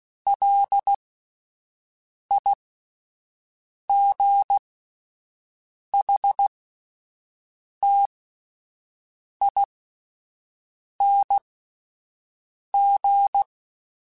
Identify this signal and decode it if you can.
Morse code, heard as a keyed tone, about 16 wpm, spaced as for 5 wpm, 800 Hz. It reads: LIGHTING
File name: Alphabetical / Alphabetical mp3s / Ls @5